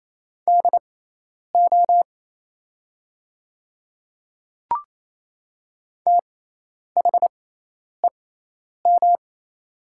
Morse code, heard as a keyed tone, about 27 wpm, spaced as for 10 wpm, 690 Hz. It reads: DO THEM
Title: 6x Character Spacing - Top 100 Words - 28wpm